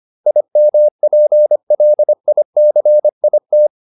Morse code, heard as a keyed tone, about 25 wpm, 600 Hz. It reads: IMPLICIT